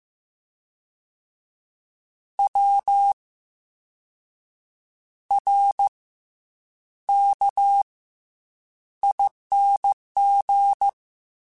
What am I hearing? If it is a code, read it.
Morse code, heard as a keyed tone, about 15 wpm, 800 Hz. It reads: W R K ING